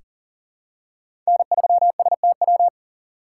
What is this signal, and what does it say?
Morse code, heard as a keyed tone, about 40 wpm, 700 Hz. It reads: N3STW
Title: USA Call Signs - 5 Characters - 6x Character Spacing - 40wpm